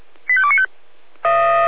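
Five-tone selective call (ZVEI), heard as quick runs of five tones, tones of about 70 ms. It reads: 85295
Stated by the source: ZVEI five-tone call 85220 received by SDR, from the server of Feuerwehr Orlamünde (fire brigade).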